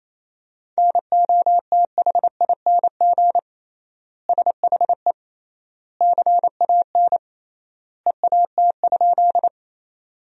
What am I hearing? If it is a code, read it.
Morse code, heard as a keyed tone, about 28 wpm, 700 Hz. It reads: NOTHING SHE CAN EAT?
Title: 3x Word-Spacing - Sentences from Top 500 Words - 28wpm